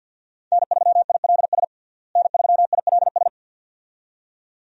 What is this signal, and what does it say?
Morse code, heard as a keyed tone, about 50 wpm, 700 Hz. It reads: N3ILS N3ILS
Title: USA Call Signs - 5 Characters - Repeated 2x - 50wpm